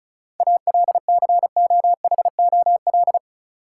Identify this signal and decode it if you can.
Morse code, heard as a keyed tone, about 34 wpm, 700 Hz. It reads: ALCOHOL
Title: Sentences from Top 1400 Words - New Word Drill - 35wpm